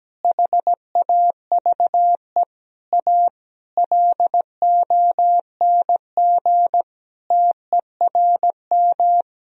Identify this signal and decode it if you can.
Morse code, heard as a keyed tone, about 17 wpm, 700 Hz. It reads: HAVE A LONG TERM
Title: Sentences from Top 1000 Words - Review Part 4 - 17wpm